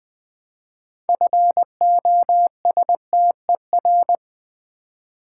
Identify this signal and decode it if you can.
Morse code, heard as a keyed tone, about 20 wpm, 700 Hz. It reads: FOSTER
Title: 6 Letter Words - First 3 Letters Given - 20wpm